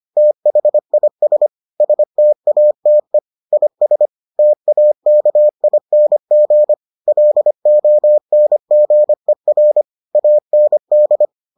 Morse code, heard as a keyed tone, about 25 wpm, 600 Hz. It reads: THIS STATE IS TAKING LONGER AND